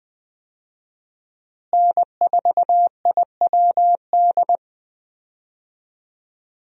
Morse code, heard as a keyed tone, about 20 wpm, 700 Hz. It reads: N4IWD
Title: USA Call Signs - 5 Characters - Repeated 3x - 20wpm